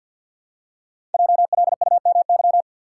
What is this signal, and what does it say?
Morse code, heard as a keyed tone, about 51 wpm, 700 Hz. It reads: JLUMY
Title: Code Groups - 5 Letters - 50wpm